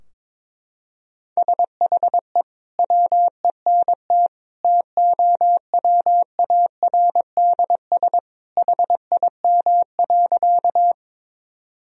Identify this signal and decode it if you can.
Morse code, heard as a keyed tone, about 22 wpm, 700 Hz. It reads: SHE WENT TOWARDS HIM.
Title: Sentences from Top 1000 Words - Review Part 2 - 22wpm